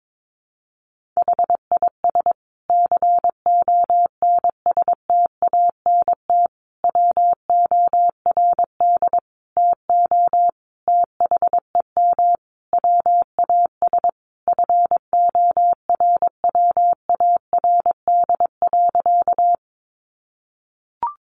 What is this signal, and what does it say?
Morse code, heard as a keyed tone, about 22 wpm, 700 Hz. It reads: HIS CONSTANT WORD TO THEM WAS FORWARD.
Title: Sentences from Top 1500 Words 22wpm